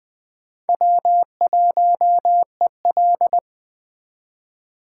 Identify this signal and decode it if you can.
Morse code, heard as a keyed tone, about 22 wpm, 700 Hz. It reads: W1EL